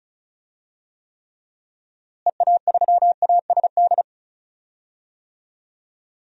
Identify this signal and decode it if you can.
Morse code, heard as a keyed tone, about 36 wpm, 700 Hz. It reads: EA3ASD
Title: DXCC-100 Call Signs - 6 Characters - Repeated 3x - 35wpm